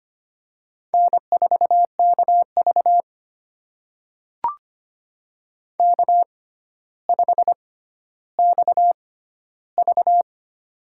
Morse code, heard as a keyed tone, about 25 wpm, spaced as for 13 wpm, 700 Hz. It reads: N4KV K5XV
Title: USA Call Signs - 4 Characters - 6x Character Spacing - 25wpm